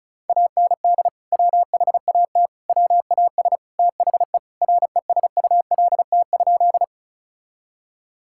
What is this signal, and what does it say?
Morse code, heard as a keyed tone, about 36 wpm, 700 Hz. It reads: AND WHAT WAS THE RESULT?
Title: Sentences from Top 1000 Words - Review Part 5 - 35wpm